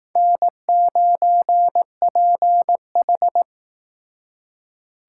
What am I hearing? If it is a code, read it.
Morse code, heard as a keyed tone, about 18 wpm, 700 Hz. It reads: N9PH
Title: Speed Racing - USA 1x2 Call Signs 15wpm